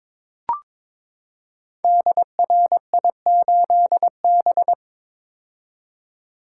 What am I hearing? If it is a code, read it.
Morse code, heard as a keyed tone, about 22 wpm, 700 Hz. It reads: DRI8B
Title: Code Groups - 5 Letter:Numbers - 22wpm